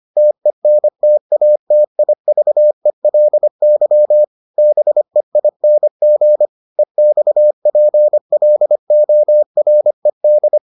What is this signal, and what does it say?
Morse code, heard as a keyed tone, about 25 wpm, 600 Hz. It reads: TENTATIVELY BEING EXPLORED